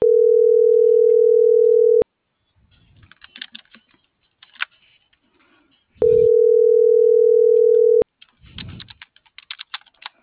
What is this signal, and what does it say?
Ringback tone (North America), 440 + 480 Hz.